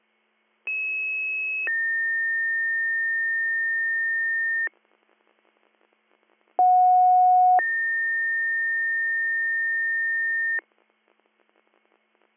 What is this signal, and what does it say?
Two-tone page, heard as a one-second tone, then a three-second tone, about 2600 then 1800 Hz, about 730 then 1800 Hz. It is heard from 0.7 to 10.6 s.